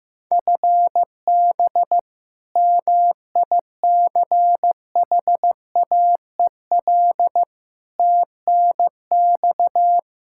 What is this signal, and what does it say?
Morse code, heard as a keyed tone, about 16 wpm, 700 Hz. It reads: FB MICHAEL TNX
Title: Practice QSO - Exchange 3 - More Info - 15wpm